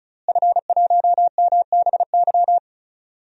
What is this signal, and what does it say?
Morse code, heard as a keyed tone, about 36 wpm, 700 Hz. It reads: F1MBY
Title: DXCC-100 Call Signs - 5 Characters - Repeated 3x - 35wpm